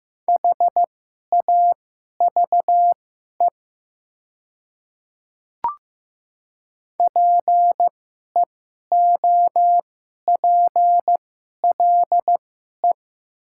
Morse code, heard as a keyed tone, about 15 wpm, spaced as for 11 wpm, 690 Hz. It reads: HAVE PEOPLE
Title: 2x Character Spacing - Top 100 Words - 15wpm